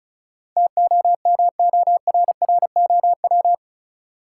Morse code, heard as a keyed tone, about 34 wpm, 700 Hz. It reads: TOMORROW
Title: Sentences from Top 1100 Words - New Word Drill - 35wpm